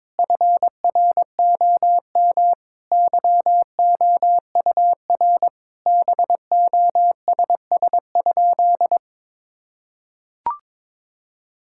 Morse code, heard as a keyed tone, about 22 wpm, 700 Hz. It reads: FROM YOUR BOSS?
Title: Sentences from Top 1000 Words - Review Part 6 - 22wpm